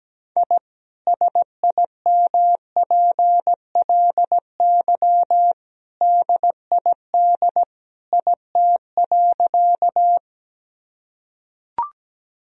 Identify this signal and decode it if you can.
Morse code, heard as a keyed tone, about 17 wpm, 700 Hz. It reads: I SIMPLY DID IT.
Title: Sentences from Top 1000 Words - Review Part 4 - 17wpm